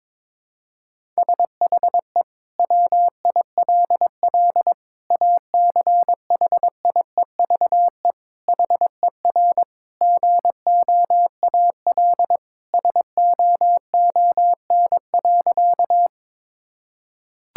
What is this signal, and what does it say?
Morse code, heard as a keyed tone, about 22 wpm, 700 Hz. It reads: SHE WILL ACHIEVE HER GOAL SOON.
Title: Sentences from Top 1000 Words - Review Part 6 - 22wpm